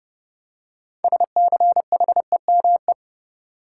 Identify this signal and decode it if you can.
Morse code, heard as a keyed tone, about 32 wpm, 700 Hz. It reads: SCHEME
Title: Six Letter Words - 30wpm